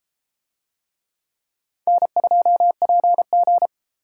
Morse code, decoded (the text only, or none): N2PG